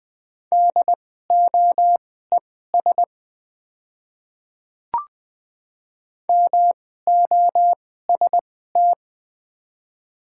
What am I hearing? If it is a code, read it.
Morse code, heard as a keyed tone, about 20 wpm, spaced as for 15 wpm, 690 Hz. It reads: DOES MOST
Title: 2x Character Spacing - Top 200 Words - 20wpm